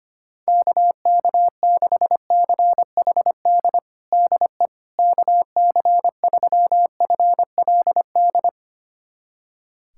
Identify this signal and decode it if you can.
Morse code, heard as a keyed tone, about 25 wpm, 700 Hz. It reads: KK6CHD DE KC3FLD
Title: USA Callsign DE Callsign - 6 Characters - 25wpm